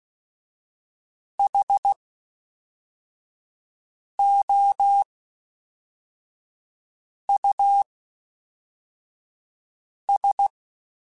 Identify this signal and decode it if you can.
Morse code, heard as a keyed tone, about 16 wpm, 800 Hz. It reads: H O U S
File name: Special INGs / Special INGs-mp3s / Fast ing @10